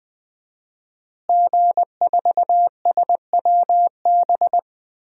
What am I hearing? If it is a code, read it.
Morse code, heard as a keyed tone, about 21 wpm, 700 Hz. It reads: G4SWB
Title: DXCC-100 Call Signs - 5 Characters - Repeated 2x - 20wpm